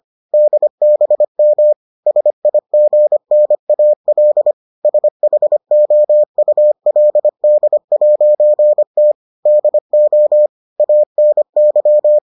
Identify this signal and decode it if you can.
Morse code, heard as a keyed tone, about 25 wpm, 600 Hz. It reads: DBM SIGNAL SHOULD'T DO ANY